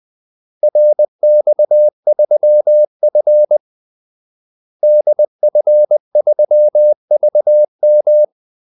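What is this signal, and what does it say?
Morse code, heard as a keyed tone, about 20 wpm, 600 Hz. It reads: RX3F DF3VM
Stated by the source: Morse call-sign practice at 20 wpm.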